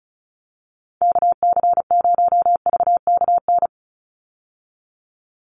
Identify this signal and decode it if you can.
Morse code, heard as a keyed tone, about 34 wpm, 700 Hz. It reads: KC0VKN
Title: Call Signs Repeated 3x - 35wpm